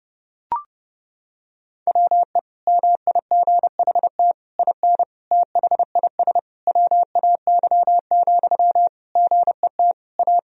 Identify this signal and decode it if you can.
Morse code, heard as a keyed tone, about 30 wpm, 700 Hz. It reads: WE MIGHT IN THIS WAY, GET A
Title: Sentences from Top 1000 Words - Review Part 1 - 30wpm